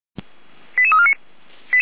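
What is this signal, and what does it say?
Five-tone selective call (ZVEI), heard as quick runs of five tones, tones of about 70 ms. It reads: 88349
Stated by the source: ZVEI five-tone call received by SDR.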